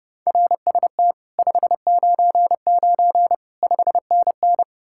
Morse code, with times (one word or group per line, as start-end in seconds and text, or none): 0.3-1.1 RST
1.4-3.3 599
3.6-4.6 5NN